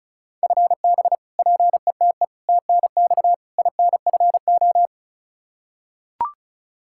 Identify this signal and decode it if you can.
Morse code, heard as a keyed tone, about 34 wpm, 700 Hz. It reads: FB PETE TNX INFO
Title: Practice QSO - Exchange 3 - More Info - 35wpm